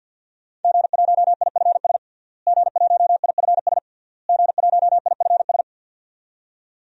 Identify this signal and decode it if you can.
Morse code, heard as a keyed tone, about 52 wpm, 700 Hz. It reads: G1IFS G1IFS G1IFS
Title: DXCC-100 Call Signs - 5 Characters - Repeated 3x - 50wpm